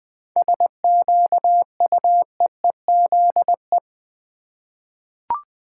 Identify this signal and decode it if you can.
Morse code, heard as a keyed tone, about 20 wpm, 690 Hz. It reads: SQUEEZE
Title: 7 Letter Words - First 3 Letters Given - 20wpm